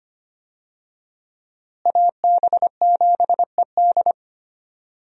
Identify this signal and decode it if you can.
Morse code, heard as a keyed tone, about 25 wpm, 700 Hz. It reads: AB7ED
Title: USA Call Signs - 5 Characters - Repeated 3x - 25wpm